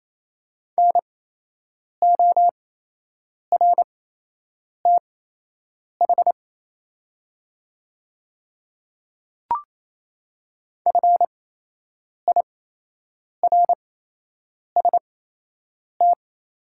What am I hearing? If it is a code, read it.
Morse code, heard as a keyed tone, about 28 wpm, spaced as for 8 wpm, 700 Hz. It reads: NORTH FIRST
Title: 8x Character Spacing - Top 300 Words - 28wpm